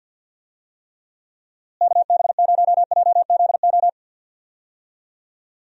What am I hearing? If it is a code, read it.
Morse code, heard as a keyed tone, about 52 wpm, 700 Hz. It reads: KB0JZO
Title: USA Call Signs - 6 Characters - Repeated 2x - 50wpm